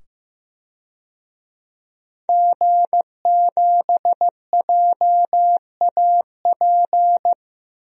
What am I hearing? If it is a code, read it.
Morse code, heard as a keyed tone, about 15 wpm, 700 Hz. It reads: G7JAP
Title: DXCC-100 Call Signs - 5 Characters - 15wpm